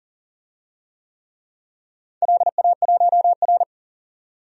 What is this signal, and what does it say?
Morse code, heard as a keyed tone, about 38 wpm, 700 Hz. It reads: LA1R
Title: DXCC-100 Call Signs - 4 Characters - 40wpm